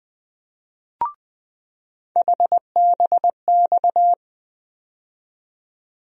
Morse code, heard as a keyed tone, about 20 wpm, 700 Hz. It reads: HBX
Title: USA Call Sign Postfixes - 3 Letters 20wpm